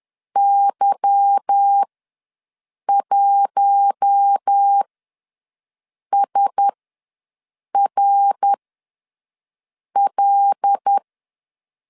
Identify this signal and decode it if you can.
Morse code, heard as a keyed tone, about 10 wpm, spaced as for 5 wpm, 800 Hz. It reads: Y1SRL
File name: LEZIONI / 9 2867 ws 10wpm 4215